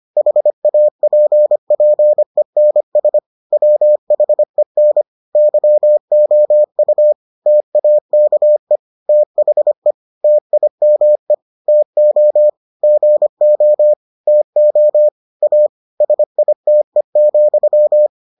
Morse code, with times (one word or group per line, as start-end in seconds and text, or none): 0.2-3.2 HAPPENS
3.5-5.0 WHEN
5.4-7.1 YOU
7.5-8.8 TAKE
9.1-9.9 THE
10.2-11.3 TIME
11.7-12.5 TO
12.8-13.9 GO
14.3-15.1 TO
15.4-15.7 A
16.0-18.1 SITE,